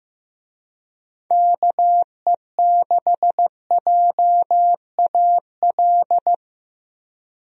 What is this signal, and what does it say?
Morse code, heard as a keyed tone, about 15 wpm, 700 Hz. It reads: KE6JAL
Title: USA Call Signs - 6 Characters - Repeated 2x - 15wpm